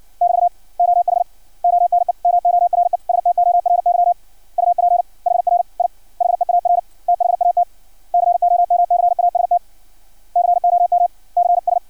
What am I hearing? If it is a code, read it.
Morse code, heard as a keyed tone, about 56 wpm, 700 Hz. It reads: <BT> OH ONE MORE STORY UP HR I HEAR THAT COMCAST COMCS
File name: archived / CRSnet